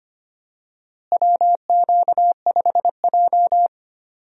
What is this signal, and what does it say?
Morse code, heard as a keyed tone, about 25 wpm, 700 Hz. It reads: WQ5J
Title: USA Call Signs - 4 Characters - Repeated 2x - 25wpm